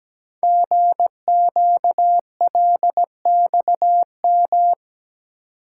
Morse code, heard as a keyed tone, about 17 wpm, 700 Hz. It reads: GQLXM
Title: Code Groups - 5 Letters - 17wpm